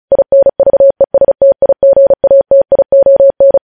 Morse code, heard as a keyed tone, about 36 wpm, 550 Hz. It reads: INVESTIGATION
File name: podcast / News headlines-1x-35x35